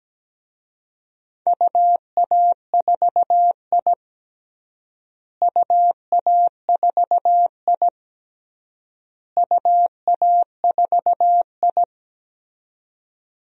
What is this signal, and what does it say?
Morse code, heard as a keyed tone, about 17 wpm, 700 Hz. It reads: UA4I UA4I UA4I
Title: DXCC-100 Call Signs - 4 Characters - Repeated 3x - 17wpm